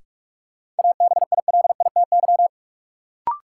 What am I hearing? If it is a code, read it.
Morse code, heard as a keyed tone, about 42 wpm, 700 Hz. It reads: ABILITY
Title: Sets of 3 Words - New Word Drill - 45wpm